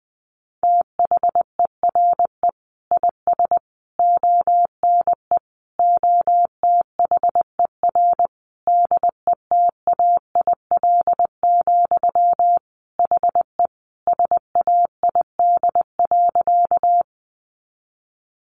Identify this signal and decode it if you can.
Morse code, heard as a keyed tone, about 20 wpm, 700 Hz. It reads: THERE IS ONE OTHER DETAIL, HE SAID.
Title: Sentences from Top 1500 Words 20wpm